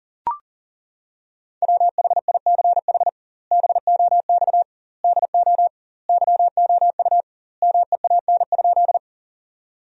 Morse code, heard as a keyed tone, about 41 wpm, 700 Hz. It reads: WHICH BOX DO YOU MEAN?
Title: 2x Word-Spacing - Sentences from Top 500 Words - 40wpm